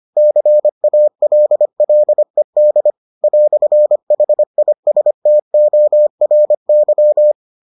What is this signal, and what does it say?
Morse code, heard as a keyed tone, about 25 wpm, 600 Hz. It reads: CALLED "HISTORY